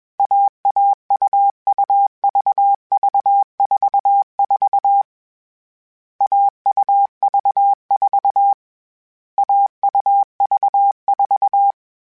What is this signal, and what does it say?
Morse code, heard as a keyed tone, about 21 wpm, 800 Hz. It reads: AAUUVV44 AUV4 AUV4
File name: Short Runs / Short Runs-mp3s / AUV AUV4 @20